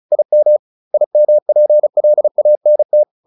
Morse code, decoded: IM IMPLANT

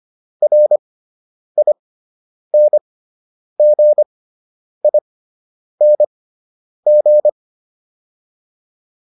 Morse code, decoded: RINGING

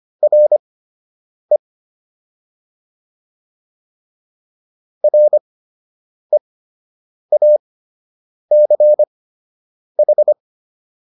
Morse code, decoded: RE REACH

